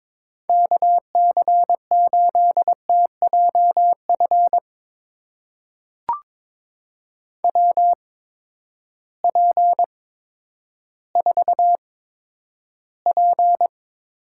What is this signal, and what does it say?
Morse code, heard as a keyed tone, about 22 wpm, spaced as for 11 wpm, 700 Hz. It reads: KC8TJF WP4P
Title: USA Call Signs - 6 Characters - 8x Character Spacing - 22wpm